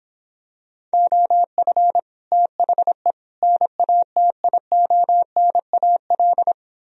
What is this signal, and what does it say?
Morse code, heard as a keyed tone, about 26 wpm, 700 Hz. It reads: OF THE NATIONAL